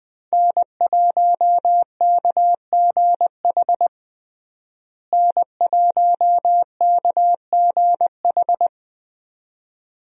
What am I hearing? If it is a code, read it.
Morse code, heard as a keyed tone, about 20 wpm, 700 Hz. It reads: N1KGH N1KGH
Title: USA Call Signs - 5 Characters - Repeated 2x - 20wpm